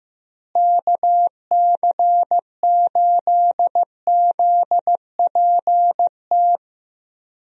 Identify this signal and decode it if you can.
Morse code, heard as a keyed tone, about 16 wpm, 700 Hz. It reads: KC8ZPT
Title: USA Call Signs - 6 Characters - Repeated 3x - 15wpm